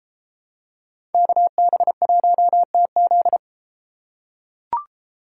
Morse code, heard as a keyed tone, about 30 wpm, 700 Hz. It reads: KB1TZ